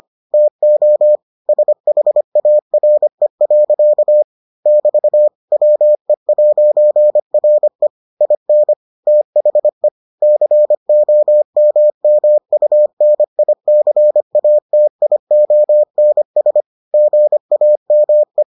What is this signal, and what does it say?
Morse code, heard as a keyed tone, about 25 wpm, 600 Hz. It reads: TO SHARE. <BT> WE'RE IN THE COMMUNICATIONS GAME